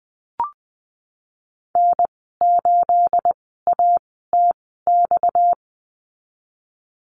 Morse code, heard as a keyed tone, about 19 wpm, spaced as for 14 wpm, 700 Hz. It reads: N8ATX
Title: 231 Lesson 37g - Letter x - Call Signs 20wpm